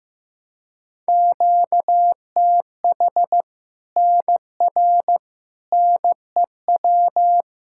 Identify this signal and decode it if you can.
Morse code, heard as a keyed tone, about 16 wpm, 700 Hz. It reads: QTH NR NEW
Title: Practice QSO - Exchange 1 - Beginning - 15wpm